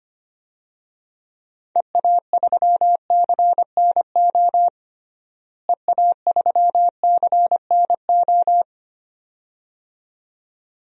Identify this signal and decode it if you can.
Morse code, heard as a keyed tone, about 25 wpm, 700 Hz. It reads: EA3CNO EA3CNO
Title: DXCC-100 Call Signs - 6 Characters - Repeated 2x - 25wpm